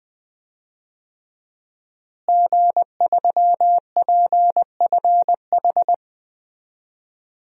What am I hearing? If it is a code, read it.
Morse code, heard as a keyed tone, about 21 wpm, 700 Hz. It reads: G3PFH
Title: DXCC-100 Call Signs - 5 Characters - Repeated 3x - 20wpm